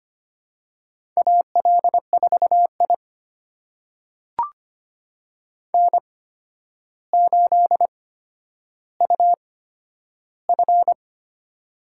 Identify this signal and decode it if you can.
Morse code, heard as a keyed tone, about 25 wpm, spaced as for 11 wpm, 700 Hz. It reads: AL4I N8UF
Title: USA Call Signs - 4 Characters - 8x Character Spacing - 25wpm